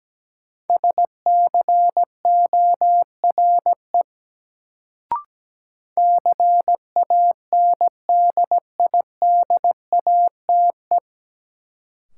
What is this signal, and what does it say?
Morse code, heard as a keyed tone, about 17 wpm, 700 Hz. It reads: SCORE CANDIDATE
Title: Sentences from Top 1600 Words - New Word Drill Alt - 17wpm